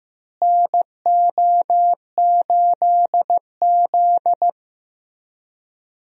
Morse code, heard as a keyed tone, about 16 wpm, 700 Hz. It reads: NO8Z
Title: USA Call Signs - 4 Characters - Repeated 2x - 15wpm